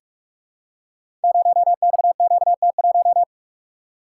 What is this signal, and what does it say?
Morse code, heard as a keyed tone, about 44 wpm, 700 Hz. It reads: 0XQT1
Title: Code Groups - 5 Letter:Numbers - 45wpm